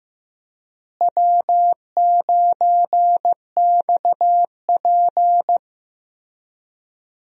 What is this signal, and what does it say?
Morse code, heard as a keyed tone, about 16 wpm, 700 Hz. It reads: W9XP
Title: USA Call Signs - 4 Characters - Repeated 3x - 15wpm